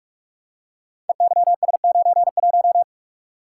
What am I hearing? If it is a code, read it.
Morse code, heard as a keyed tone, about 44 wpm, 700 Hz. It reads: EYS91
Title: Code Groups - 5 Letter:Numbers - 45wpm